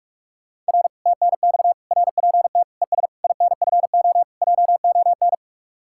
Morse code, heard as a keyed tone, about 66 wpm, spaced as for 55 wpm, 700 Hz. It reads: R TNX RPT ES INFO JON
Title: Speed-Racing Common QSO Phrases Encore - 45wpm